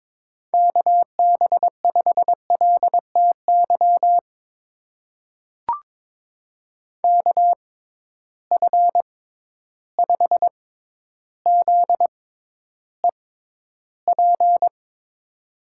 Morse code, heard as a keyed tone, about 22 wpm, spaced as for 11 wpm, 700 Hz. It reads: KB5LTY KF5ZEP
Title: USA Call Signs - 6 Characters - 6x Character Spacing - 22wpm